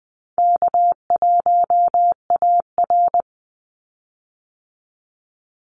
Morse code, heard as a keyed tone, about 20 wpm, 700 Hz. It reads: K1AR
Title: Easy Call Signs Repeated 3x - 20wpm